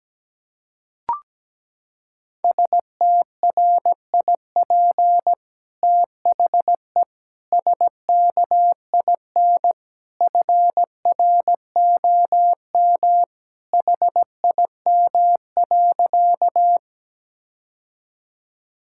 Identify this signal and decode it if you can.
Morse code, heard as a keyed tone, about 17 wpm, 700 Hz. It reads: STRIP THE SKIN FROM HIM.
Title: Sentences from Top 1600 Words 17wpm